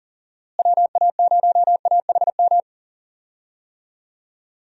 Morse code, heard as a keyed tone, about 40 wpm, 700 Hz. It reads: WA0AHM
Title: USA Call Signs - 6 Characters - Repeated 3x - 40wpm